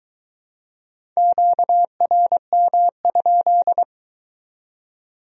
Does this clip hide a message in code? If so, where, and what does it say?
Yes: Morse code from 1.2 to 3.8 s, heard as a keyed tone, about 20 wpm, 700 Hz. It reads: QRM?